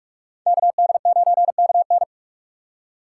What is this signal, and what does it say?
Morse code, heard as a keyed tone, about 59 wpm, 700 Hz. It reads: KD9KN